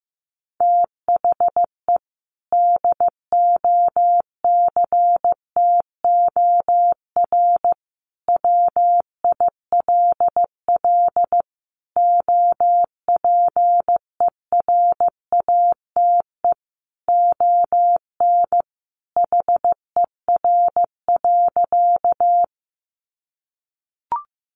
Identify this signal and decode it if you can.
Morse code, heard as a keyed tone, about 15 wpm, 700 Hz. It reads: THE DOCTOR WILL OPERATE ON HER.
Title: Sentences from Top 1500 Words 15wpm